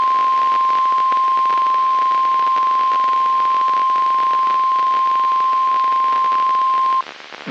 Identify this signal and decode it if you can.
Emergency Alert System (SAME) tones (the 1050 Hz attention tone). It is heard from 0.3 to 6.9 s.
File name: eas messages / EAS WXM95 Towanda